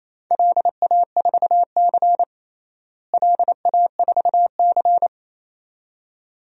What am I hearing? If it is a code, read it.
Morse code, heard as a keyed tone, about 28 wpm, 700 Hz. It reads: LA4C LA4C
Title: DXCC-100 Call Signs - 4 Characters - Repeated 2x - 28wpm